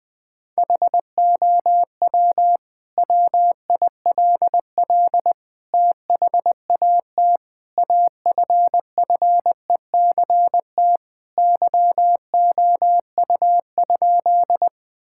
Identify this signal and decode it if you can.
Morse code, heard as a keyed tone, about 20 wpm, 700 Hz. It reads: HOW WILL THAT AFFECT YOU?
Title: Sentences from Top 1000 Words - Review Part 6 - 20wpm